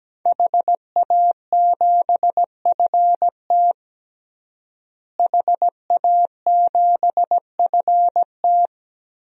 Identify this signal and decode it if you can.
Morse code, heard as a keyed tone, about 17 wpm, 700 Hz. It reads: HA7FT HA7FT
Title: DXCC-100 Call Signs - 5 Characters - Repeated 3x - 17wpm